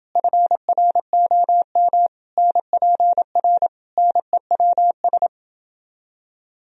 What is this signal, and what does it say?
Morse code, heard as a keyed tone, about 29 wpm, 700 Hz. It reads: FROM NPR NEWS